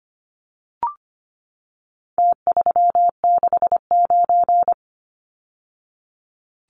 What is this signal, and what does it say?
Morse code, heard as a keyed tone, about 25 wpm, 700 Hz. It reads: T369
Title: Numbers - Contest Style - 4-Digit 25wpm